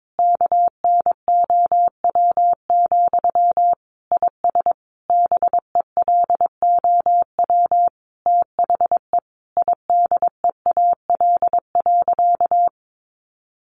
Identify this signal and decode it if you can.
Morse code, heard as a keyed tone, about 22 wpm, 700 Hz. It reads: KNOW, IS BELOW THE IDEAL.
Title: Sentences from Top 1400 Words 22wpm